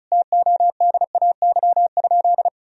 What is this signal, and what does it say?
Morse code, heard as a keyed tone, about 34 wpm, 700 Hz. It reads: TODAY?
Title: Sentences from Top 1000 Words - Review Part 4 - 35wpm